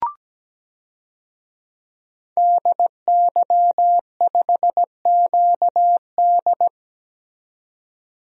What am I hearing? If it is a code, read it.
Morse code, heard as a keyed tone, about 17 wpm, 700 Hz. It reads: DY5QD
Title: DXCC-100 Call Signs - 5 Characters - 17wpm